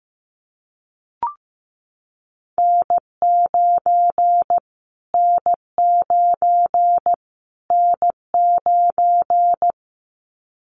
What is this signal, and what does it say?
Morse code, heard as a keyed tone, about 15 wpm, 700 Hz. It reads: N9 N9 N9
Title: Callsign Prefixes for Top 25 DXCC - Repeated 3x - 15wpm